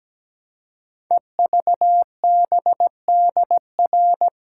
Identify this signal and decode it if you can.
Morse code, heard as a keyed tone, about 17 wpm, 700 Hz. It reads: EVBDR
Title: Code Groups - 5 Letter:Numbers - 17wpm